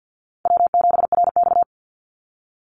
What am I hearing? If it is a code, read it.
Morse code, heard as a keyed tone, about 50 wpm, 700 Hz. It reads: W7RX
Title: Call Signs - 50wpm